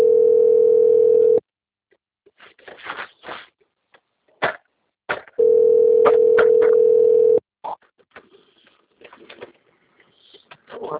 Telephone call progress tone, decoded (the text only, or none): ringback tone